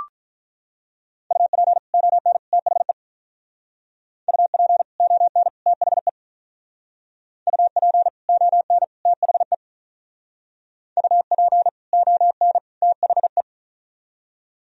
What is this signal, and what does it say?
Morse code, heard as a keyed tone, about 41 wpm, 700 Hz. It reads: UP ON THE UP ON THE UP ON THE UP ON THE
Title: Speed-Racing - Sets of 3 Words 35wpm